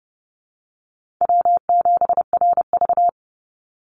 Morse code, heard as a keyed tone, about 30 wpm, 700 Hz. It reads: W7RV
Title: NA SOTA Chaser Callsigns 30wpm